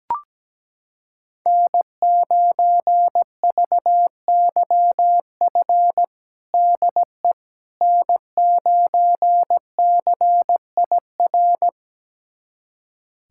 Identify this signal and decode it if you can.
Morse code, heard as a keyed tone, about 17 wpm, 700 Hz. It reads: N9VYF DE N9CIR
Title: USA Callsign DE Callsign - 5 Characters - 17wpm